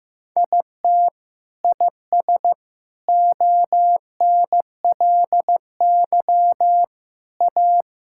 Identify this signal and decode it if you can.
Morse code, heard as a keyed tone, about 15 wpm, 700 Hz. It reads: IT IS ONLY A
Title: Sentences from Top 1600 Words 15wpm